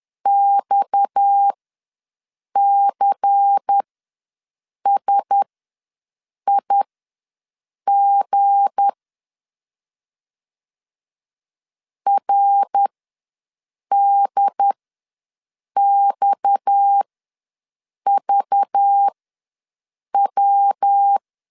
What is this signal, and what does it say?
Morse code, heard as a keyed tone, about 10 wpm, spaced as for 6 wpm, 800 Hz. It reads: XCSIG RDXVW
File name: LEZIONI / 7 GF9Z / 7 G ws 10wpm 3843